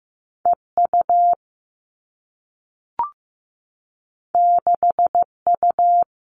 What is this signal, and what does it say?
Morse code, heard as a keyed tone, about 15 wpm, 700 Hz. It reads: EU 6U
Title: Two Challenging Letter-Number Combinations 15wpm